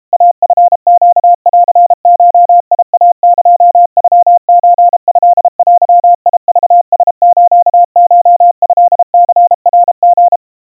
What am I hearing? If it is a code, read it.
Morse code (Japanese Wabun code), heard as a keyed tone, about 32 wpm, 700 Hz. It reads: イチネンゴイエノソトデクラスコトニナリ
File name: kawa / Morse sample / kamyu 120 cpm 700 Hz Crispy